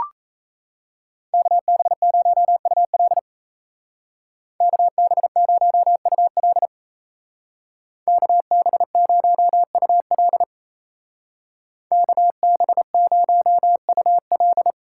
Morse code, heard as a keyed tone, about 34 wpm, 700 Hz. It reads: KB0UL KB0UL KB0UL KB0UL
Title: Speed Racing - USA 2x2 Call Signs 28wpm